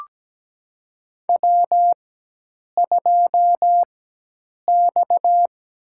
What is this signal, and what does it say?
Morse code, heard as a keyed tone, about 17 wpm, spaced as for 8 wpm, 700 Hz. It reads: W2X
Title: USA Call Signs - 4x Character Spacing - 17wpm